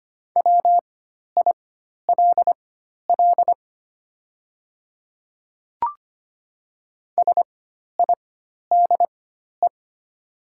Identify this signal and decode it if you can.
Morse code, heard as a keyed tone, about 25 wpm, spaced as for 12 wpm, 700 Hz. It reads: WILL SIDE
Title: 4x Character Spacing - Top 200 Words - 25wpm